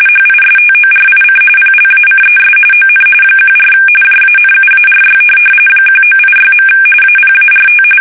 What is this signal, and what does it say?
RTTY, about 45.45 baud. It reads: HISTORIC PLACES ON 14 JANUARY 1986. APART
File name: fsk850 / USS-Massachusetts